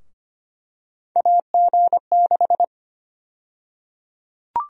Morse code, heard as a keyed tone, about 25 wpm, 700 Hz. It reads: AG6
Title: USA Call Sign Prefixes - Repeated 2x - 25wpm